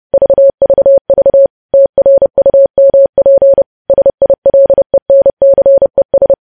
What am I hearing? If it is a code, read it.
Morse code, heard as a keyed tone, about 30 wpm, 550 Hz. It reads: VVV TRUMP SILENCES
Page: News Headlines in Morse Code at 30 WPM